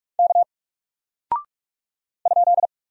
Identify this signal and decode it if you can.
Morse code, heard as a keyed tone, about 44 wpm, 700 Hz. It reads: K ?
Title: Single Character (Letter:Number:Punctuation:Prosign) Alt 45wpm